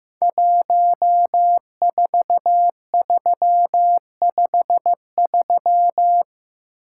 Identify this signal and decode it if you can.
Morse code, heard as a keyed tone, about 15 wpm, 700 Hz. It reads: 14353
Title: Code Groups - 5 Letter:Numbers - 15wpm